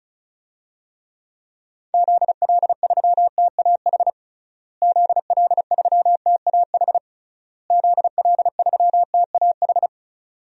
Morse code, heard as a keyed tone, about 36 wpm, 700 Hz. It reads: ZL3TAH ZL3TAH ZL3TAH
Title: DXCC-100 Call Signs - 6 Characters - Repeated 3x - 35wpm